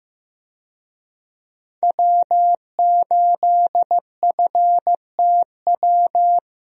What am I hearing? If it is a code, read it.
Morse code, heard as a keyed tone, about 15 wpm, 700 Hz. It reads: W8FTW
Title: USA Call Signs - 5 Characters - Repeated 2x - 15wpm